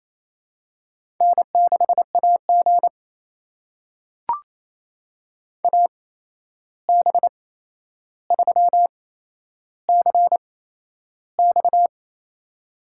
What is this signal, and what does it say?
Morse code, heard as a keyed tone, about 28 wpm, spaced as for 11 wpm, 700 Hz. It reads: N6AG AB3CX
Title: USA Call Signs - 8x Character Spacing - 28wpm